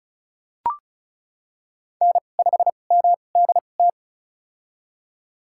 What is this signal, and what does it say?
Morse code, heard as a keyed tone, about 36 wpm, spaced as for 25 wpm, 700 Hz. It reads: N5MDT
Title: USA Call Signs - 5 Characters - 2x Character Spacing - 35wpm